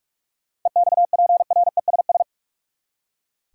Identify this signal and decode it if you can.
Morse code, heard as a keyed tone, about 44 wpm, 700 Hz. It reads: EXPRESS
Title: Seven Letter Words - 45wpm